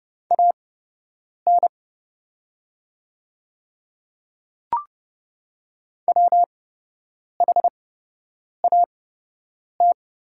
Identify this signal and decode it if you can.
Morse code, heard as a keyed tone, about 30 wpm, spaced as for 8 wpm, 690 Hz. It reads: AN WHAT
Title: 8x Character Spacing - Top 200 Words - 30wpm